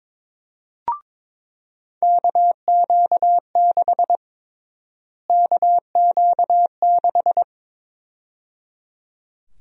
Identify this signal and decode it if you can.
Morse code, heard as a keyed tone, about 21 wpm, 700 Hz. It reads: KQ6 KQ6
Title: USA Call Sign Prefixes - Repeated 2x - 22wpm